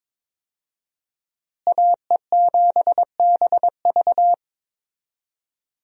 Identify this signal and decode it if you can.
Morse code, heard as a keyed tone, about 22 wpm, 700 Hz. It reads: AE7BV
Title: USA Call Signs - 5 Characters - Repeated 3x - 22wpm